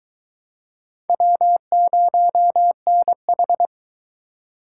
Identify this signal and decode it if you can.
Morse code, heard as a keyed tone, about 22 wpm, 700 Hz. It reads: W0NH